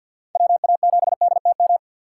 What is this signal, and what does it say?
Morse code, heard as a keyed tone, about 47 wpm, 700 Hz. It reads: WA7DTM
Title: USA Call Signs - 2x Character Spacing - 50wpm